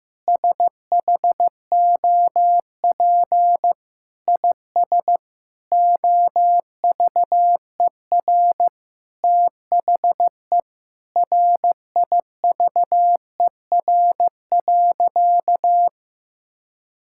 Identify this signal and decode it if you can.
Morse code, heard as a keyed tone, about 15 wpm, 700 Hz. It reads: SHOP IS OVER THE RIVER.
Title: Sentences from Top 1600 Words 15wpm